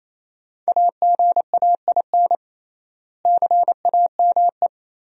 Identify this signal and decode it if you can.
Morse code, heard as a keyed tone, about 28 wpm, 700 Hz. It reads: AGAIN CAME
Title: 3x Word-Spacing - Sentences from Top 500 Words - 28wpm